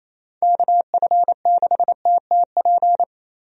Morse code, heard as a keyed tone, about 28 wpm, 700 Hz. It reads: KF6TTP
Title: USA Call Signs - 6 Characters - Repeated 3x - 28wpm